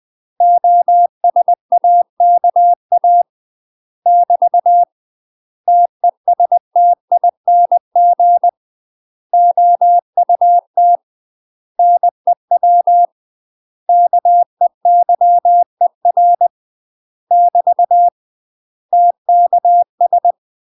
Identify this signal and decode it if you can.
Morse code, heard as a keyed tone, about 20 wpm, 700 Hz. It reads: OSAKA <BT> TESTING OUT NEW KEYER <BT> TKS
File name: UK MASTER QSOs TRAINING / 20WPM / UK MASTER QSOs 0060